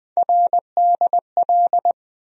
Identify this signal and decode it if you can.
Morse code, heard as a keyed tone, about 20 wpm, 700 Hz. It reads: RDL
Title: USA Call Sign Postfixes - 3 Letters 20wpm